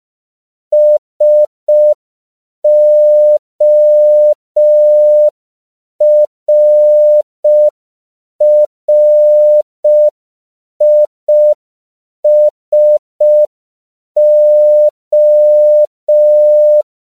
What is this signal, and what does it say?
Morse code, heard as a keyed tone, about 5 wpm, 600 Hz. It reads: SORRISO
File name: 5PPM